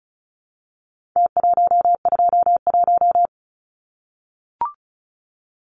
Morse code, heard as a keyed tone, about 34 wpm, 700 Hz. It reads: T121
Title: Numbers - Contest Style - 4-Digit 35wpm